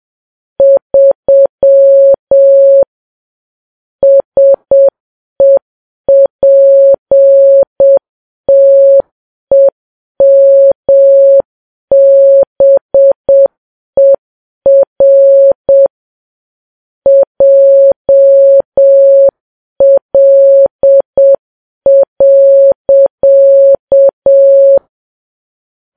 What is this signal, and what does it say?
Morse code, heard as a keyed tone, about 7 wpm, 550 Hz. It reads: 3 SEPTEMBER JL.
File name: Files Web tls / Morse on9fv / morse7kps